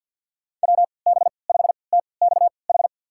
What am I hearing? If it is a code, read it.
Morse code, heard as a keyed tone, about 52 wpm, spaced as for 29 wpm, 700 Hz. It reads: WB5TXH
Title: USA Call Signs - 6 Characters - 3x Character Spacing - 50wpm